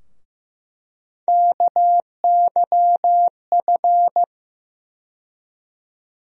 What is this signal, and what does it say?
Morse code, heard as a keyed tone, about 15 wpm, 700 Hz. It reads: KYF
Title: USA Call Sign Postfixes - 3 Letters 15wpm